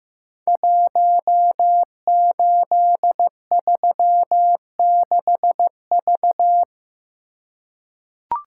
Morse code, heard as a keyed tone, about 15 wpm, 700 Hz. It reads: 1836V
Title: Code Groups - 5 Letter:Numbers - 15wpm